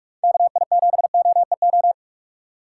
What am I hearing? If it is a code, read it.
Morse code, heard as a keyed tone, about 46 wpm, 700 Hz. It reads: KI7OEO
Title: USA Call Signs - 6 Characters - 2x Character Spacing - 45wpm